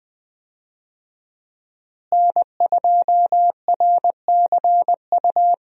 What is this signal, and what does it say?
Morse code, heard as a keyed tone, about 20 wpm, 700 Hz. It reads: N2RCU
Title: USA Call Signs - 5 Characters - Repeated 2x - 20wpm